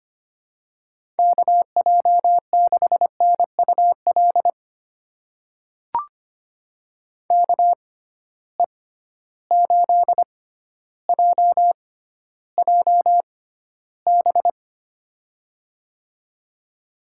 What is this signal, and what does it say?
Morse code, heard as a keyed tone, about 25 wpm, spaced as for 13 wpm, 700 Hz. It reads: KJ6NUL KE8JJB
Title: USA Call Signs - 6 Characters - 6x Character Spacing - 25wpm